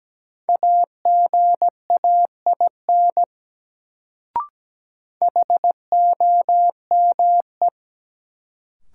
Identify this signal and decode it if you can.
Morse code, heard as a keyed tone, about 17 wpm, 700 Hz. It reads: AGAIN HOME
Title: Sentences from Top 200 Words - New Word Drill - 17wpm